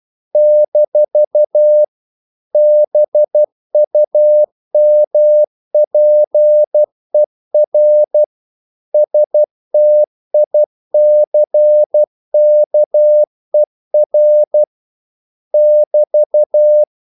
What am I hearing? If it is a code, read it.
Morse code, heard as a keyed tone, about 12 wpm, 600 Hz. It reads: - BUMPER STICKER <BT>
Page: English quotations in CW (telegraphy - morse)